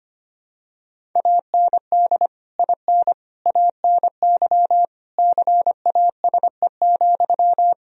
Morse code, heard as a keyed tone, about 25 wpm, 700 Hz. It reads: AND IN ANY CASE,
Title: Sentences from Top 1000 Words - Review Part 1 - 25wpm